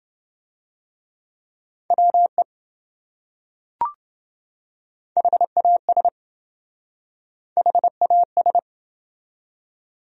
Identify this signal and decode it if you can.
Morse code, heard as a keyed tone, about 27 wpm, 700 Hz. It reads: WE HAS HAS